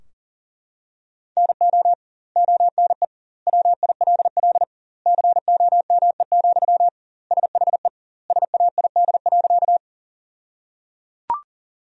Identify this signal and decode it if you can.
Morse code, heard as a keyed tone, about 37 wpm, 700 Hz. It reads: NO ONE WILL COME, SHE SAID.
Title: 2x Word-Spacing - Sentences from Top 100 Words - 40wpm